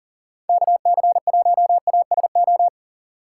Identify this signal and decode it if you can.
Morse code, heard as a keyed tone, about 40 wpm, 700 Hz. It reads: KC1ASO
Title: USA Call Signs - 2x Character Spacing - 40wpm